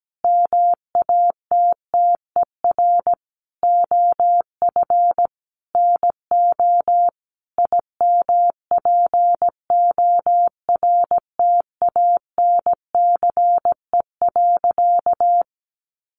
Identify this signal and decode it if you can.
Morse code, heard as a keyed tone, about 17 wpm, 700 Hz. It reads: MATTER OF NO IMPORTANCE.
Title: Sentences from Top 1200 Words 17wpm